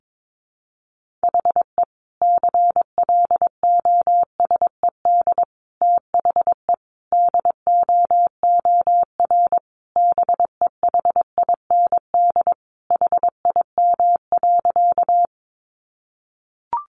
Morse code, heard as a keyed tone, about 22 wpm, 700 Hz. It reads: HE CLOSED THE DOOR BEHIND HIM.
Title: Sentences from Top 1400 Words 22wpm